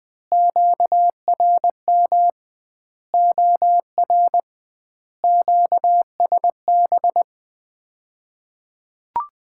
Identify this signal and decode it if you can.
Morse code, heard as a keyed tone, about 21 wpm, 700 Hz. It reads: QRM OR QSB
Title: Common QSO Phrases - 2x Word-Spacing - 20wpm